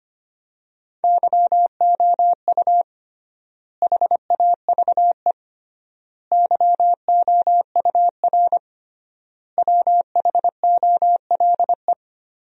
Morse code, heard as a keyed tone, about 25 wpm, 700 Hz. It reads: YOU HAVE YOUR WHOLE
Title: 3x Word-Spacing - Sentences from Top 400 Words - 25wpm